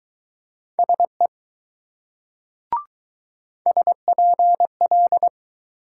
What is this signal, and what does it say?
Morse code, heard as a keyed tone, about 19 wpm, 700 Hz. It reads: SE SPL